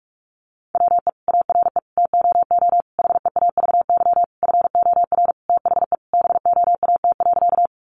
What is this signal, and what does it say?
Morse code, heard as a keyed tone, about 44 wpm, 700 Hz. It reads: WE ARE TOO HEAVY FOR THE BOAT.
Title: Sentences from Top 1100 Words 45wpm